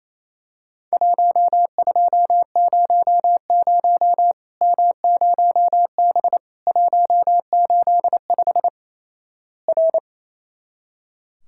Morse code, heard as a keyed tone, about 28 wpm, 700 Hz. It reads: 1200 M0B 185 R
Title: Worked All Europe DX Contest - QTC Exchange - 28wpm